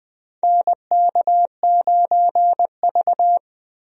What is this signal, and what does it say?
Morse code, heard as a keyed tone, about 21 wpm, 700 Hz. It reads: NK9V
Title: USA Call Signs - 4 Characters - 20wpm